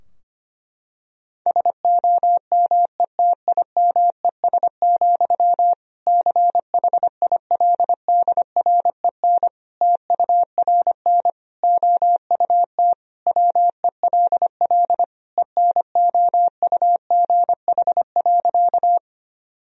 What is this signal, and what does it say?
Morse code, heard as a keyed tone, about 25 wpm, 700 Hz. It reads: SOMETIMES, CHILDREN TURN OUT WELL ENOUGH.
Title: Sentences from Top 1000 Words - Review Part 3 - 25wpm